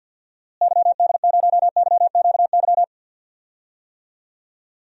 Morse code, heard as a keyed tone, about 51 wpm, 700 Hz. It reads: YD0YQY
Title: DXCC-100 Call Signs - 6 Characters - Repeated 2x - 50wpm